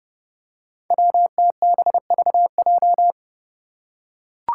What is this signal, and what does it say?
Morse code, heard as a keyed tone, about 30 wpm, 700 Hz. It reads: WTBVJ
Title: Five Challenging Letter-Number Combinations 30wpm